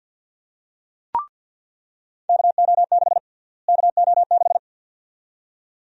Morse code, heard as a keyed tone, about 49 wpm, 700 Hz. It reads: KO6 KO6
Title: USA Call Sign Prefixes - Repeated 2x - 50wpm